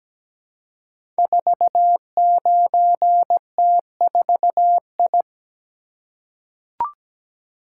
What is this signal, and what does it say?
Morse code, heard as a keyed tone, about 17 wpm, 700 Hz. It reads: 49T4I
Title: Code Groups - 5 Letter:Numbers - 17wpm